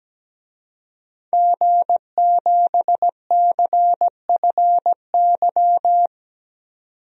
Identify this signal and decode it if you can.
Morse code, heard as a keyed tone, about 17 wpm, 700 Hz. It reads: G7CFY
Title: DXCC-100 Call Signs - 5 Characters - 17wpm